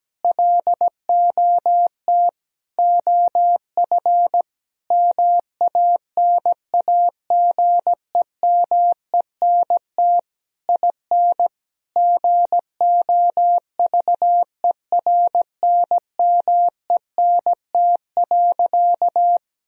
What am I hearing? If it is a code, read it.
Morse code, heard as a keyed tone, about 17 wpm, 700 Hz. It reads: LOT OF MANAGEMENT IN GOVERNMENT.
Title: Sentences from Top 1000 Words - Review Part 4 - 17wpm